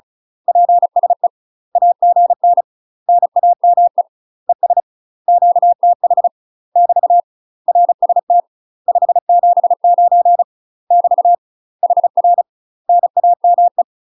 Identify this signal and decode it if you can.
Morse code, heard as a keyed tone, about 34 wpm, 700 Hz. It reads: PSE AGN NAME ES QTH <BT> RST 579 <BT> HR NAME
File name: UK MASTER QSOs TRAINING / 35WPM / UK MASTER QSOs 0025